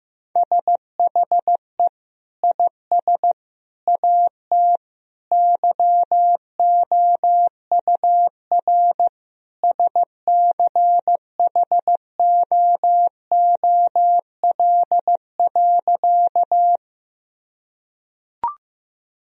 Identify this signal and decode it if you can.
Morse code, heard as a keyed tone, about 15 wpm, 700 Hz. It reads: SHE IS AT YOUR SCHOOL.
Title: Sentences from Top 1000 Words - Review Part 1 - 15wpm